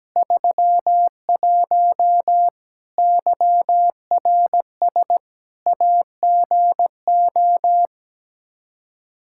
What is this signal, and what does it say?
Morse code, heard as a keyed tone, about 17 wpm, 700 Hz. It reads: 31 YRS AGO
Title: Practice QSO - Exchange 3 - More Info - 17wpm